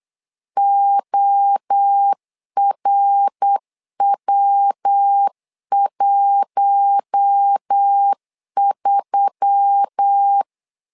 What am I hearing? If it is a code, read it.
Morse code, heard as a keyed tone, about 8 wpm, 800 Hz. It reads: ORW13
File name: LEZIONI / 4 Q1S3 8wpm 3037